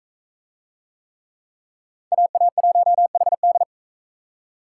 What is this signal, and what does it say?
Morse code, heard as a keyed tone, about 43 wpm, 700 Hz. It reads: AA1HD